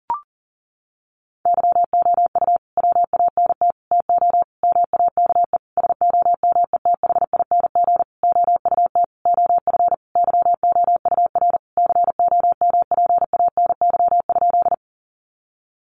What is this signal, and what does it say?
Morse code, heard as a keyed tone, about 40 wpm, 700 Hz. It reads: YOU WANT TO MAKE SOMETHING OUT OF YOUR COMPANY?
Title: Sentences from Top 600 Words 40wpm